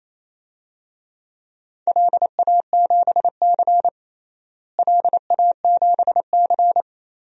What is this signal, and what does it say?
Morse code, heard as a keyed tone, about 28 wpm, 700 Hz. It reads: LA7C LA7C
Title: DXCC-100 Call Signs - 4 Characters - Repeated 3x - 28wpm